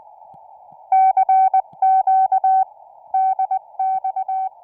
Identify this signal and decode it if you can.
Morse code, heard as a keyed tone, about 20 wpm, 760 Hz. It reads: CQ DX